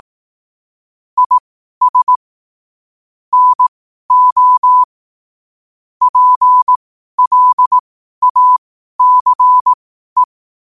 Morse code, heard as a keyed tone, about 18 wpm, spaced as for 12 wpm, 1000 Hz. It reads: IS NO PLACE
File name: qotdfiles / 12wpm